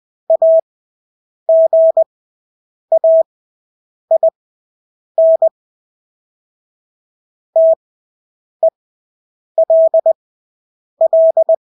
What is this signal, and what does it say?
Morse code, heard as a keyed tone, about 20 wpm, spaced as for 8 wpm, 650 Hz. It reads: AGAIN TELL
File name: Engelske ord 20/8 wpm